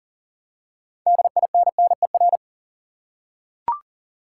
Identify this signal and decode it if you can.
Morse code, heard as a keyed tone, about 40 wpm, 700 Hz. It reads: DINNER
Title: Six Letter Words - 40wpm